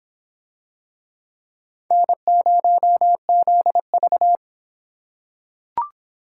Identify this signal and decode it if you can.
Morse code, heard as a keyed tone, about 25 wpm, 700 Hz. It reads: N0ZV